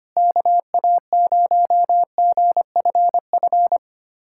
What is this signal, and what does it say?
Morse code, heard as a keyed tone, about 25 wpm, 700 Hz. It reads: KA0GFF
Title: USA Callsign DE Callsign 25wpm